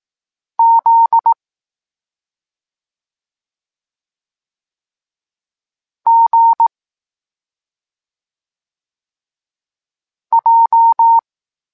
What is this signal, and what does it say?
Morse code, heard as a keyed tone, about 18 wpm, 920 Hz. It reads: Z G J
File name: Tekst-26 1